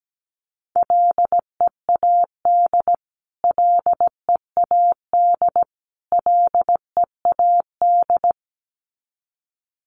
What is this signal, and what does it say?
Morse code, heard as a keyed tone, about 17 wpm, 700 Hz. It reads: LEAD LEAD LEAD
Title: Top 500 Words Repeated 17wpm